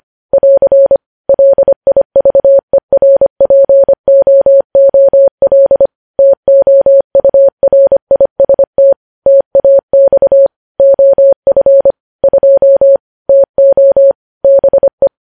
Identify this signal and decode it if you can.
Morse code, heard as a keyed tone, about 25 wpm, 550 Hz. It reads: <AR> LIVERPOOL TOURIST TAX OF 2 TO BE
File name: podcast / News headlines-1x-25x25